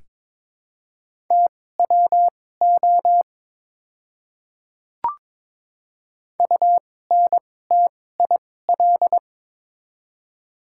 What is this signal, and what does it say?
Morse code, heard as a keyed tone, about 23 wpm, spaced as for 16 wpm, 700 Hz. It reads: TWO UNTIL
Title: 2x Character Spacing - Top 500 Words - 22wpm